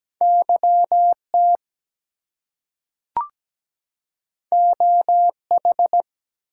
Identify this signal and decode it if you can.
Morse code, heard as a keyed tone, about 17 wpm, 700 Hz. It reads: YT OH
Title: US States and Canadian Provinces 17wpm